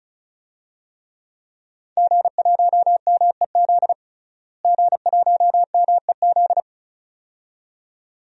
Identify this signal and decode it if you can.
Morse code, heard as a keyed tone, about 36 wpm, 700 Hz. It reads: G1MEZ G1MEZ
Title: DXCC-100 Call Signs - 5 Characters - Repeated 2x - 35wpm